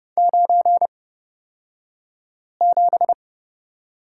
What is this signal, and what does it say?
Morse code, heard as a keyed tone, about 30 wpm, spaced as for 5 wpm, 700 Hz. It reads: 97
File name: Lesson 37d - Letter x - Two Characters 30wpm